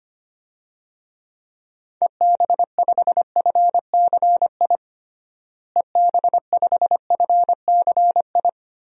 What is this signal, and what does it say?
Morse code, heard as a keyed tone, about 25 wpm, 700 Hz. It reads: EB5FCI EB5FCI
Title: DXCC-100 Call Signs - 6 Characters - Repeated 3x - 25wpm